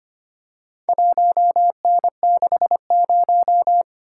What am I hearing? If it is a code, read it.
Morse code, heard as a keyed tone, about 25 wpm, 700 Hz. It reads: 1N60
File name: practice / 25wpm / Four Letter-Number Combinations - 25wpm